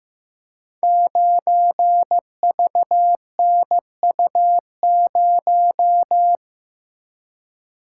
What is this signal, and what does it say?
Morse code, heard as a keyed tone, about 15 wpm, 700 Hz. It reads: 9VNU0
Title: Code Groups - 5 Letter:Numbers - 15wpm